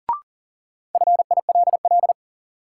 Morse code, heard as a keyed tone, about 40 wpm, 700 Hz. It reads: FILL
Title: Sentences from Top 1000 Words - Review Part 3 - New Word Drill - 40wpm